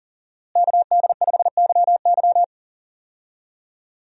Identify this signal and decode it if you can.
Morse code, heard as a keyed tone, about 40 wpm, 700 Hz. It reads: KD5YY
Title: USA Call Signs - 5 Characters - Repeated 2x - 40wpm